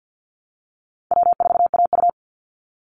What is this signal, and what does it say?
Morse code, heard as a keyed tone, about 50 wpm, 700 Hz. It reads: W4AU
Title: Easy Call Signs Repeated 3x - 50wpm